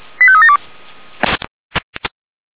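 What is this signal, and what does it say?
Five-tone selective call (ZVEI), heard as quick runs of five tones, tones of about 70 ms. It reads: 85372
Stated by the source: SDR receiver recording of ZVEI five-tone selective call 85371.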